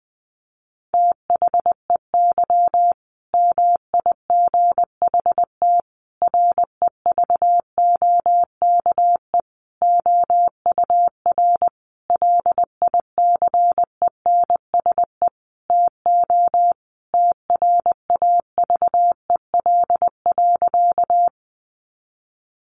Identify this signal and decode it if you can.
Morse code, heard as a keyed tone, about 20 wpm, 700 Hz. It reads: THEY MIGHT REVOKE OUR LICENSE TO TRAVEL.
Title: A Sound Of Thunder 20wpm